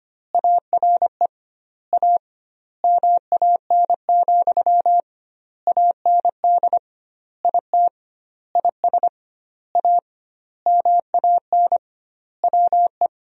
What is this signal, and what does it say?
Morse code, heard as a keyed tone, about 25 wpm, 700 Hz. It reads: ARE A MAN, AND IT IS A MAN WE
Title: 2x Word-Spacing - Sentences from Top 200 Words - 25wpm